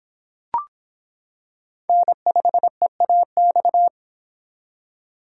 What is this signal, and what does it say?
Morse code, heard as a keyed tone, about 26 wpm, 700 Hz. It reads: N5EAX